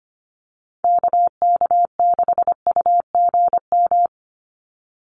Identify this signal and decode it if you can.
Morse code, heard as a keyed tone, about 25 wpm, 700 Hz. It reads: KK6UGM
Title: USA Call Signs 25wpm